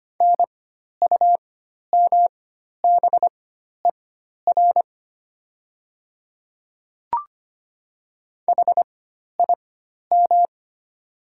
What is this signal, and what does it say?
Morse code, heard as a keyed tone, about 25 wpm, spaced as for 12 wpm, 700 Hz. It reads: NUMBER HIM
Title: 4x Character Spacing - Top 200 Words - 25wpm